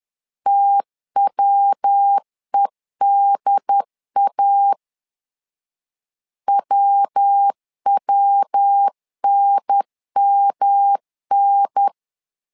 Morse code, heard as a keyed tone, about 10 wpm, 800 Hz. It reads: TWEDA WWNMN
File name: LEZIONI / 3 MWRO / 3 MW 10wpm 1959